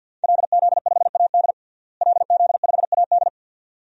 Morse code, heard as a keyed tone, about 51 wpm, 700 Hz. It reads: LZ5AD LZ5AD
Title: DXCC-100 Call Signs - 5 Characters - Repeated 2x - 50wpm